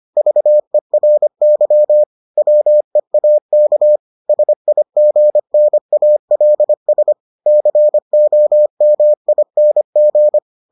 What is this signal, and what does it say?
Morse code, heard as a keyed tone, about 25 wpm, 600 Hz. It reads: VERY WEAK SIGNALS COMING